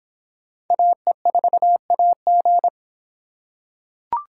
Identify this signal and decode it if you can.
Morse code, heard as a keyed tone, about 23 wpm, 700 Hz. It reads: AE4AG